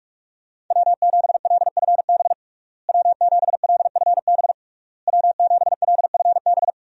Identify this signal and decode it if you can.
Morse code, heard as a keyed tone, about 46 wpm, 700 Hz. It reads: W7LFB W7LFB W7LFB
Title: USA Call Signs - 5 Characters - Repeated 3x - 45wpm